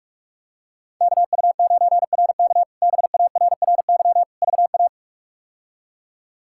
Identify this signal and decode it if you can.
Morse code, heard as a keyed tone, about 46 wpm, 700 Hz. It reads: KU9RK BARRY VA
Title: K1USN SST QSO Practice - 45wpm